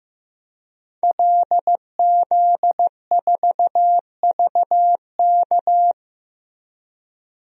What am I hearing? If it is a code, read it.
Morse code, heard as a keyed tone, about 15 wpm, 700 Hz. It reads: LZ4VK
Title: DXCC-100 Call Signs - 5 Characters - Repeated 3x - 15wpm